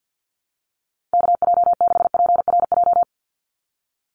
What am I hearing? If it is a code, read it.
Morse code, heard as a keyed tone, about 50 wpm, 700 Hz. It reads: KJ6PRJ
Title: USA Call Signs 50wpm